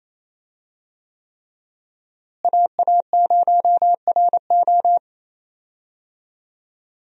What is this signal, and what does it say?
Morse code, heard as a keyed tone, about 28 wpm, 700 Hz. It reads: AA0RO
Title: USA Call Signs - 5 Characters - Repeated 3x - 28wpm